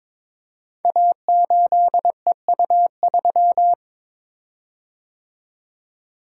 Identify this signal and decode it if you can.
Morse code, heard as a keyed tone, about 22 wpm, 700 Hz. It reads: A8EU3
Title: Code Groups - 5 Letter:Numbers - 22wpm